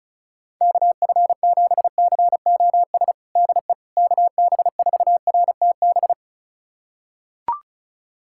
Morse code, heard as a keyed tone, about 36 wpm, 700 Hz. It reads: KF7COS DE KB4RTB
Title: USA Callsign DE Callsign - 6 Characters - 35wpm